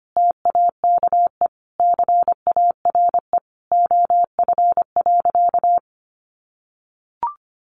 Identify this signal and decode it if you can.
Morse code, heard as a keyed tone, about 25 wpm, 700 Hz. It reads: TAKE CARE OF.
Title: Additional Easy Phrases 25wpm